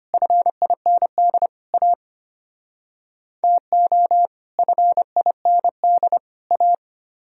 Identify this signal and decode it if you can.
Morse code, heard as a keyed tone, about 33 wpm, 700 Hz. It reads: FIND A TO FIND A